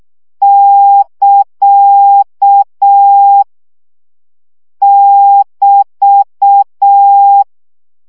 Morse code, heard as a keyed tone, about 6 wpm, 800 Hz. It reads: <KA> <BT>
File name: CW gyakorlas ct 5 baud